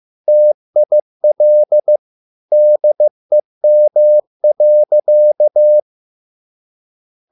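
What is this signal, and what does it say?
Morse code, heard as a keyed tone, about 15 wpm, 600 Hz. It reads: TIL DEM.
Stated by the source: Morse code at a slow 15 wpm from CW med Gnister.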